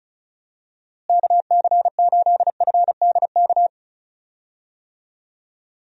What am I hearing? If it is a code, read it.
Morse code, heard as a keyed tone, about 36 wpm, 700 Hz. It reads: KC8FDK
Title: USA Call Signs - 6 Characters - Repeated 3x - 35wpm